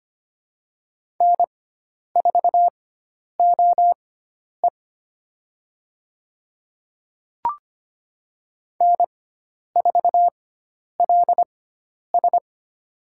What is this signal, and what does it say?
Morse code, heard as a keyed tone, about 25 wpm, spaced as for 10 wpm, 700 Hz. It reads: N4OE N4LS
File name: Lesson 16g - Letter c - Call Signs 25wpm